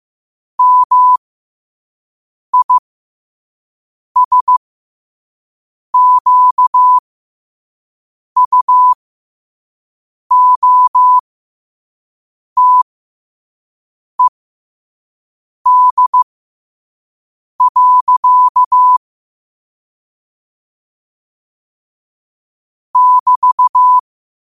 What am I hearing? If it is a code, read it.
Morse code, heard as a keyed tone, about 15 wpm, spaced as for 5 wpm, 1000 Hz. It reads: MISQUOTED. <BT>
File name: qotdfiles / 5wpm